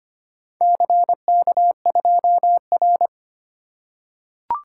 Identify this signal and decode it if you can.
Morse code, heard as a keyed tone, about 25 wpm, 700 Hz. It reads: CK2R
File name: practice / 25wpm / Four Letter-Number Combinations - 25wpm